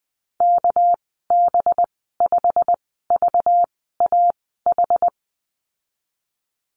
Morse code, heard as a keyed tone, about 20 wpm, spaced as for 15 wpm, 700 Hz. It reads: KB5VAH
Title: 238 Lesson 38g - Letter q - Call Signs 20wpm